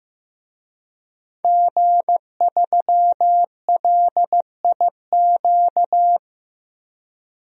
Morse code, heard as a keyed tone, about 15 wpm, 700 Hz. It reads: G3LIQ
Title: DXCC-100 Call Signs - 5 Characters - Repeated 3x - 15wpm